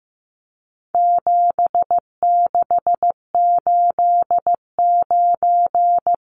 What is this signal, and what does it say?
Morse code, heard as a keyed tone, about 15 wpm, 700 Hz. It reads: 7689
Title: Numbers - 4-Digit 15wpm